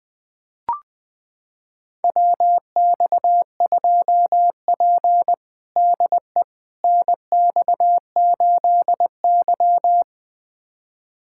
Morse code, heard as a keyed tone, about 21 wpm, 700 Hz. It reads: WX2P DE NX8Y
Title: USA Callsign DE Callsign - 4 Characters - 20wpm